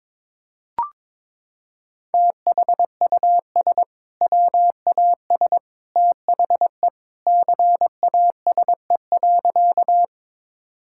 Morse code, heard as a keyed tone, about 22 wpm, 700 Hz. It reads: THUS WAS THE CASE.
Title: Sentences from Top 1000 Words - Review Part 5 - 22wpm